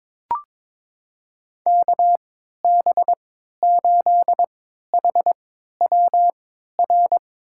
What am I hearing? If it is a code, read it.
Morse code, heard as a keyed tone, about 21 wpm, spaced as for 12 wpm, 700 Hz. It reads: KB8HWR
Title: USA Call Signs - 6 Characters - 3x Character Spacing - 22wpm